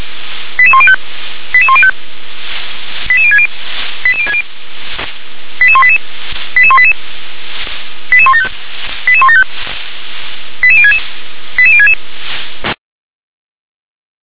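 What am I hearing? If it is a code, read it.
Five-tone selective call (ZVEI), heard as quick runs of five tones, tones of about 70 ms. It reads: 88105 88105 88060 88060 88188 88188 88175 88175 88066 88066
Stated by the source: ZVEI five-tone selective call received via SDR.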